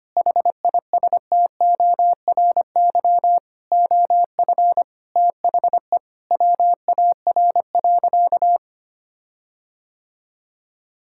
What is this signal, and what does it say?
Morse code, heard as a keyed tone, about 25 wpm, 700 Hz. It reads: HISTORY OF THE WAR.
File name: practice / 25wpm / Sentences with 4-Word Phrases - 25wpm